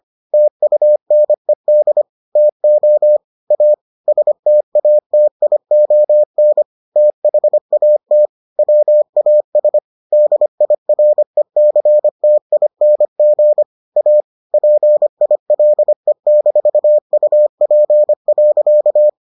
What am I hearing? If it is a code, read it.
Morse code, heard as a keyed tone, about 25 wpm, 600 Hz. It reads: TUNED TO A STATION THAT WAS DIRECTING A PILE-UP.